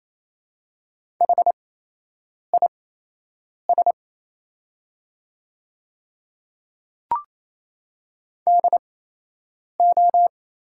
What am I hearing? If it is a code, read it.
Morse code, heard as a keyed tone, about 28 wpm, spaced as for 8 wpm, 700 Hz. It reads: HIS DO
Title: 8x Character Spacing - Top 100 Words - 28wpm